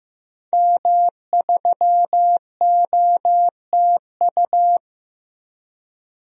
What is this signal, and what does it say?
Morse code, heard as a keyed tone, about 16 wpm, 700 Hz. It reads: M3OTU
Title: DXCC-100 Call Signs - 5 Characters - Repeated 3x - 15wpm